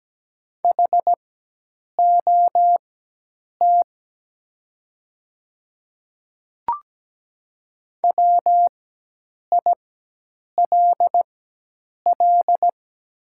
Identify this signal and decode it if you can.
Morse code, heard as a keyed tone, about 17 wpm, spaced as for 8 wpm, 690 Hz. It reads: HOT WILL
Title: 4x Character Spacing - Top 100 Words - 17wpm